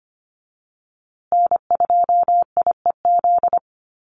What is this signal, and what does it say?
Morse code, heard as a keyed tone, about 25 wpm, 700 Hz. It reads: N2IEZ
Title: USA Call Signs Repeated 3x 25wpm